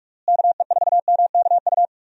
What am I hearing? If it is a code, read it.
Morse code, heard as a keyed tone, about 47 wpm, 700 Hz. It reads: KE4MKU
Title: USA Call Signs - 6 Characters - Repeated 3x - 45wpm